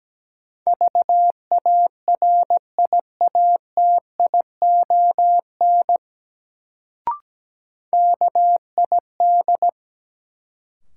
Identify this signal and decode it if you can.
Morse code, heard as a keyed tone, about 17 wpm, 700 Hz. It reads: VARIATION KID
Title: Sentences from Top 1600 Words - New Word Drill - 17wpm